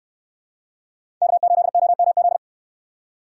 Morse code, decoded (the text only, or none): K8OMZ